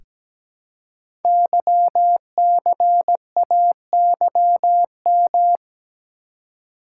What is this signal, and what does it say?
Morse code, heard as a keyed tone, about 17 wpm, 700 Hz. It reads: YCAYM
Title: Code Groups - 5 Letters - 17wpm